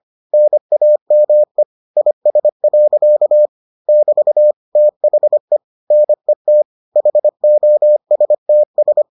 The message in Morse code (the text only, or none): NAME IS. <BT> THE NET HOSTS